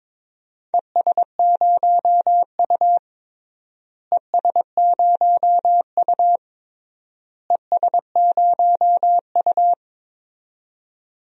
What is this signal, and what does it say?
Morse code, heard as a keyed tone, about 21 wpm, 700 Hz. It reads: ES0U ES0U ES0U
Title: DXCC-100 Call Signs - 4 Characters - Repeated 3x - 22wpm